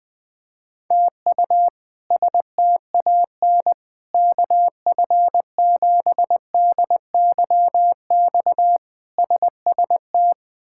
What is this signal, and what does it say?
Morse code, heard as a keyed tone, about 21 wpm, 700 Hz. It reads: TU STAN KF7DYX SST
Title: K1USN SST QSO Practice - 20wpm